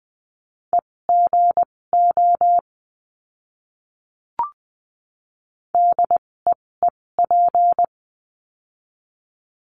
Morse code, heard as a keyed tone, about 20 wpm, spaced as for 16 wpm, 700 Hz. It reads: EGO DEEP
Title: 256 Lesson 42a - Words 12-20wpm